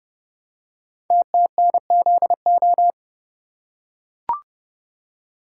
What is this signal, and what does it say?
Morse code, heard as a keyed tone, about 30 wpm, 700 Hz. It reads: TTNZO
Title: Code Groups - 5 Letters - 30wpm